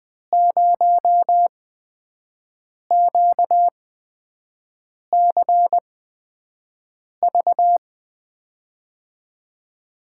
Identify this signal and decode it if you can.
Morse code, heard as a keyed tone, about 20 wpm, spaced as for 5 wpm, 700 Hz. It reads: 0QCV
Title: USA Call Signs - 5 Characters - 8x Character Spacing - 20wpm